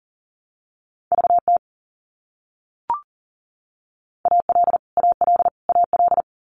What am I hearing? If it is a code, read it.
Morse code, heard as a keyed tone, about 40 wpm, 700 Hz. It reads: VT AL AL AL
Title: US States Repeated 3x 40wpm